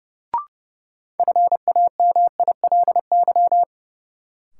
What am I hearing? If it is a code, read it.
Morse code, heard as a keyed tone, about 30 wpm, 700 Hz. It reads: FAMILY
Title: Sentences from Top 400 Words - New Word Drill - 30wpm